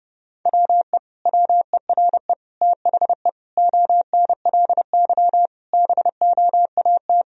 Morse code, heard as a keyed tone, about 30 wpm, 700 Hz. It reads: WE WERE THE ONLY BOAT
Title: Sentences from Top 1000 Words - Review Part 1 - 30wpm